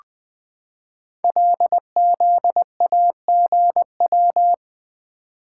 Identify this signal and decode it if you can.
Morse code, heard as a keyed tone, about 20 wpm, 700 Hz. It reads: LZAGW
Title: Code Groups - 5 Letter:Numbers - 20wpm